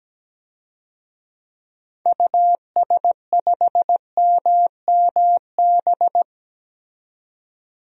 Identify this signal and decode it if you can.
Morse code, heard as a keyed tone, about 17 wpm, 700 Hz. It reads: US5MMB
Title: DXCC-100 Call Signs - 6 Characters - 17wpm